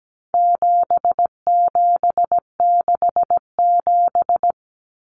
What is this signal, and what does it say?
Morse code, heard as a keyed tone, about 17 wpm, 700 Hz. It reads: 7767
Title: Numbers - 4-Digit 17wpm